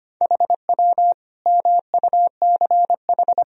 Morse code, heard as a keyed tone, about 25 wpm, 700 Hz. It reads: HW MUCH
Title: Sentences from Top 300 Words and Abbreviations 25wpm